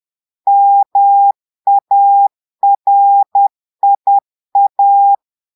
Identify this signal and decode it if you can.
Morse code, heard as a keyed tone, about 10 wpm, 800 Hz. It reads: MARIA
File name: Portugais - texte en clair 1 - vitesse - 10 mots minute